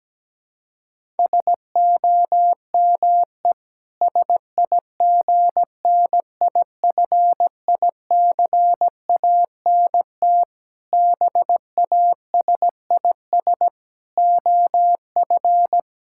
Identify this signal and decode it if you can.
Morse code, heard as a keyed tone, about 17 wpm, 700 Hz. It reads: SOME SIGNIFICANT BASIS OF
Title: Sentences from Top 1600 Words 17wpm